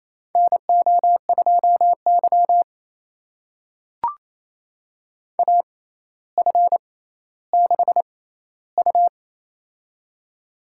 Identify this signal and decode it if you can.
Morse code, heard as a keyed tone, about 29 wpm, spaced as for 15 wpm, 700 Hz. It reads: NO2Y AF6U
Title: USA Call Signs - 4 Characters - 6x Character Spacing - 28wpm